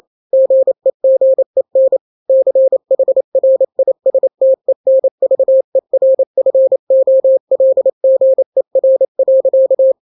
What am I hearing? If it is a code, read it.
Morse code, heard as a keyed tone, about 28 wpm, 530 Hz. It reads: GEGEN CHRISTENVERFOLGER.
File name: daily / 27wpm